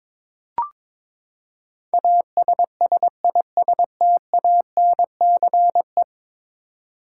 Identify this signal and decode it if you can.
Morse code, heard as a keyed tone, about 22 wpm, 700 Hz. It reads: ASSISTANCE
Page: morse-code-ninja-practice-22wpm